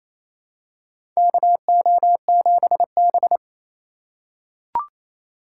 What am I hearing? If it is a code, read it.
Morse code, heard as a keyed tone, about 27 wpm, 690 Hz. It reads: KO7B
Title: USA Call Signs - 4 Characters - 2x Character Spacing - 28wpm